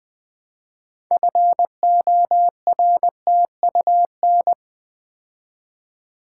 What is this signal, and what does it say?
Morse code, heard as a keyed tone, about 20 wpm, 700 Hz. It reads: FORTUN
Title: Build A Word - 3 Letters At A Time - 9-Letter Words - 20wpm